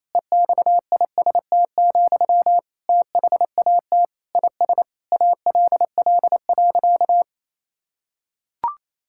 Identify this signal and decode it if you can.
Morse code, heard as a keyed tone, about 28 wpm, 700 Hz. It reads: EXIST, THAT IS ALL.
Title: Sentences from Top 1000 Words - Review Part 5 - 28wpm